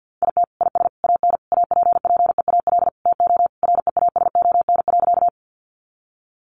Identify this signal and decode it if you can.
Morse code, heard as a keyed tone, about 50 wpm, 700 Hz. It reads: IT IS AN APPEAL TO REASON.
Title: Sentences from Top 1300 Words 50wpm